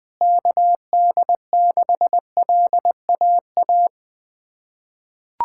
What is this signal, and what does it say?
Morse code, heard as a keyed tone, about 20 wpm, 700 Hz. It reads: KD6LAA
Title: USA Callsign DE Callsign - 6 Characters - 20wpm